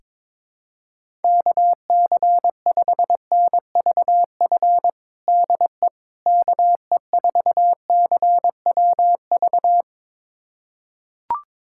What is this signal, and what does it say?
Morse code, heard as a keyed tone, about 22 wpm, 700 Hz. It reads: KC5NVF DE KE4CWV
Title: USA Callsign DE Callsign 22wpm